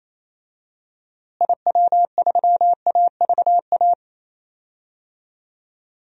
Morse code, heard as a keyed tone, about 27 wpm, 700 Hz. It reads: IW3AVA
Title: DXCC-100 Call Signs - 6 Characters - Repeated 2x - 28wpm